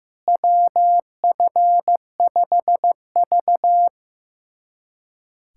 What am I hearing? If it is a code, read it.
Morse code, heard as a keyed tone, about 15 wpm, 700 Hz. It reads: WF5V
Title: USA Callsign DE Callsign - 4 Characters - 15wpm